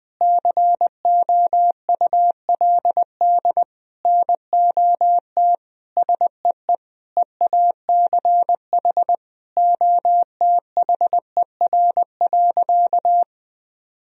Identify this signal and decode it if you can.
Morse code, heard as a keyed tone, about 20 wpm, 700 Hz. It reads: COULD NOT SEE EACH OTHER.
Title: Sentences from Top 1000 Words - Review Part 1 - 20wpm